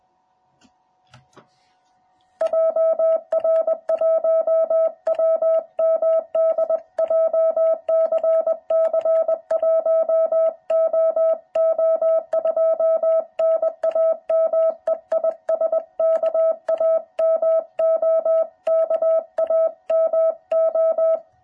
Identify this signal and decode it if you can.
Morse code, heard as a keyed tone, about 20 wpm, 660 Hz. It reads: JR1WMD JCC 1OO2 NAME IS KAMO KAMO